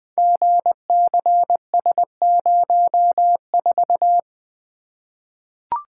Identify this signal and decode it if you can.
Morse code, heard as a keyed tone, about 20 wpm, 700 Hz. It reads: GCS04
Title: Code Groups - 5 Letter:Numbers - 20wpm